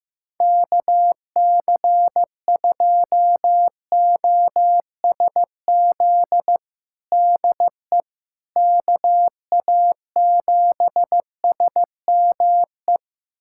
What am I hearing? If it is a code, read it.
Morse code, heard as a keyed tone, about 15 wpm, 700 Hz. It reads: KC2OSZ DE KA7SME
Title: USA Callsign DE Callsign - 6 Characters - 15wpm